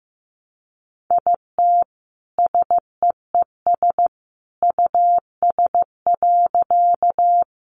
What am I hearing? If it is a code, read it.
Morse code, heard as a keyed tone, about 15 wpm, 700 Hz. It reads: IT SEES US.
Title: A Sound Of Thunder 15wpm